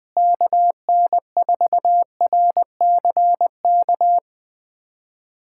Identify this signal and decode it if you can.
Morse code, heard as a keyed tone, about 20 wpm, 700 Hz. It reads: KN4RCK
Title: USA Callsign DE Callsign 20wpm